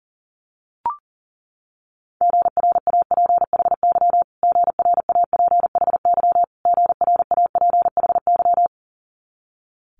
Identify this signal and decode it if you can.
Morse code, heard as a keyed tone, about 40 wpm, 700 Hz. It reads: GRAPHY GRAPHY GRAPHY
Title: Multi-letter Phonemes Repeated 3x 40wpm